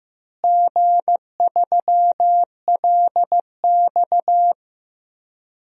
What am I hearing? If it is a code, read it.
Morse code, heard as a keyed tone, about 15 wpm, 700 Hz. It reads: G3LX
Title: DXCC-100 Call Signs - 4 Characters - Repeated 3x - 15wpm